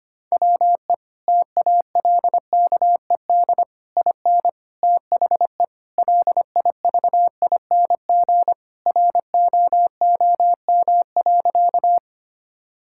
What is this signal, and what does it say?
Morse code, heard as a keyed tone, about 25 wpm, 700 Hz. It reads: WE TALKED IN THE LIVING ROOM.
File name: practice / 25wpm / Sentences with 4-Word Phrases - 25wpm